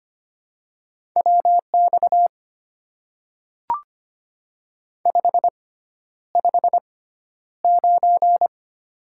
Morse code, heard as a keyed tone, about 25 wpm, spaced as for 11 wpm, 700 Hz. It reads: WX 559
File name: Top 90 QSO Elements - 6x Character Spacing - 25wpm